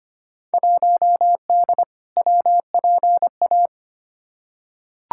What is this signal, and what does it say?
Morse code, heard as a keyed tone, about 25 wpm, 700 Hz. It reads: 1D WPA
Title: Field Day Exchanges - 25wpm